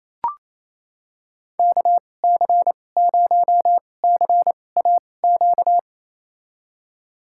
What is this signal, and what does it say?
Morse code, heard as a keyed tone, about 28 wpm, spaced as for 20 wpm, 700 Hz. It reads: KC0CAQ
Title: USA Call Signs - 6 Characters - 2x Character Spacing - 28wpm